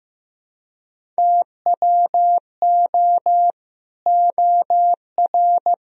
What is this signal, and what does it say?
Morse code, heard as a keyed tone, about 15 wpm, 700 Hz. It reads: TWO OR
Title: Sentences from Top 1000 Words - Review Part 1 - 15wpm